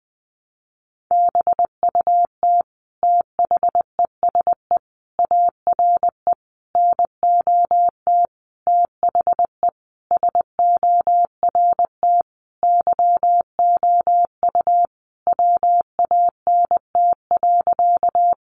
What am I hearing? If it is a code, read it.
Morse code, heard as a keyed tone, about 19 wpm, 700 Hz. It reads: BUT THESE ARE NOT THE SORT YOU WANT.
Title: Sentences from Top 1300 Words 20wpm